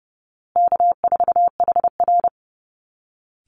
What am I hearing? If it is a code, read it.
Morse code, heard as a keyed tone, about 30 wpm, 700 Hz. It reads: K4HR
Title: Easy Call Signs Repeated 3x - 30wpm